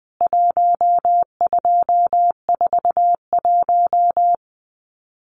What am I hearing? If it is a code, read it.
Morse code, heard as a keyed tone, about 20 wpm, 700 Hz. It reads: 1241
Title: CWT Exchange 20wpm